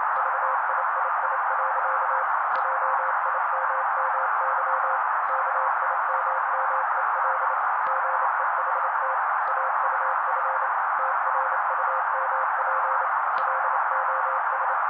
Morse code, heard as a keyed tone, about 27 wpm, 530 Hz. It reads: VIII. JIMMY KIMMEL GEHT AUF TRUMP LOS
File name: daily / 27wpm noise